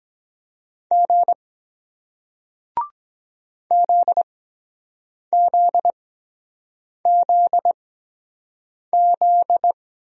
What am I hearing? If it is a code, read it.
Morse code, heard as a keyed tone, about 23 wpm, spaced as for 7 wpm, 700 Hz. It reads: G ZZZZ